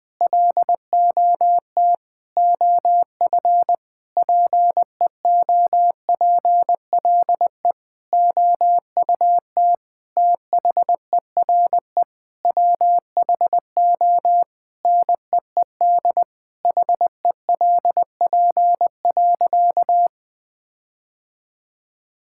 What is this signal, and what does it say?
Morse code, heard as a keyed tone, about 20 wpm, 700 Hz. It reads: LOT OF PEOPLE OUT THERE WHO NEED HELP.
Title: Sentences with 8-Word Phrases - 20wpm